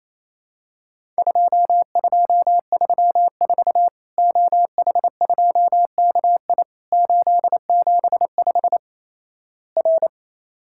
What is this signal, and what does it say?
Morse code, heard as a keyed tone, about 28 wpm, 700 Hz. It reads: 2234 OH2KI 875 R
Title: Worked All Europe DX Contest - QTC Exchange - 28wpm